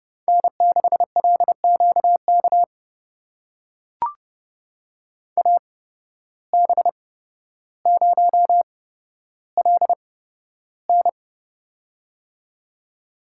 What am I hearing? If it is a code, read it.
Morse code, heard as a keyed tone, about 30 wpm, spaced as for 13 wpm, 700 Hz. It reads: N6LQK AB0LN
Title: USA Call Signs - 5 Characters - 8x Character Spacing - 30wpm